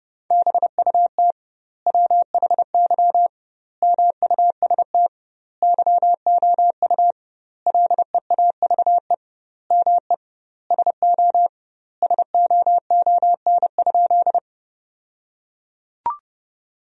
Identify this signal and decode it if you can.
Morse code, heard as a keyed tone, about 30 wpm, 700 Hz. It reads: BUT WHY MUST YOU LEAVE ME SO SOON?
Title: 2x Word-Spacing - Sentences from Top 400 Words - 30wpm